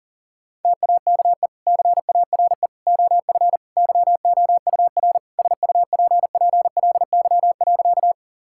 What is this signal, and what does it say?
Morse code, heard as a keyed tone, about 40 wpm, 700 Hz. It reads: TAKE CARE OF YOUR SUPPLY.
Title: Sentences from Top 1000 Words - Review Part 5 - 40wpm